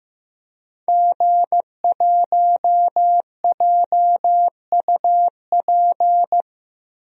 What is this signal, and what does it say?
Morse code, heard as a keyed tone, about 15 wpm, 700 Hz. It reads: G1JUP
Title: DXCC-100 Call Signs - 5 Characters - Repeated 3x - 15wpm